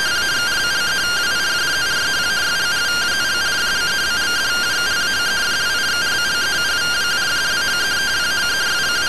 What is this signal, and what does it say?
Two-tone teleprinter signal, about 170 Hz shift.